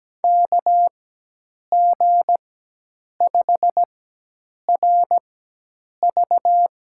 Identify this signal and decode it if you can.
Morse code, heard as a keyed tone, about 17 wpm, spaced as for 8 wpm, 700 Hz. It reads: KG5RV
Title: USA Call Signs - 4x Character Spacing - 17wpm